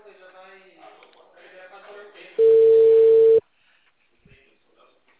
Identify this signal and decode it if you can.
Ringback tone (China, 450 Hz), with speech.
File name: out- -2051- - - .72124